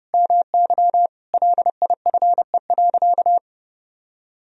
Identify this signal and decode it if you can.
Morse code, heard as a keyed tone, about 30 wpm, 700 Hz. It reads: MY LIFE.
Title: Sentences from Top 1000 Words - Review Part 3 - 30wpm